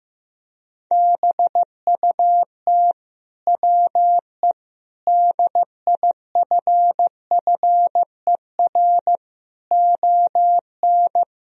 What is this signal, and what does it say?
Morse code, heard as a keyed tone, about 15 wpm, 700 Hz. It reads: BUT WE DIFFER ON
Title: Sentences from Top 1000 Words - Review Part 1 - 15wpm